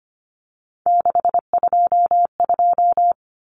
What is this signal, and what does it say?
Morse code, heard as a keyed tone, about 25 wpm, 700 Hz. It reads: 622
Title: Common Cut Numbers - 3-Digit 25wpm